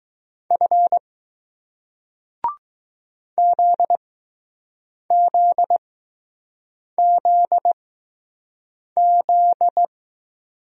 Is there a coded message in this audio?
Yes: Morse code from 0.5 to 9.8 s, heard as a keyed tone, about 21 wpm, 700 Hz. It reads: F Z Z Z Z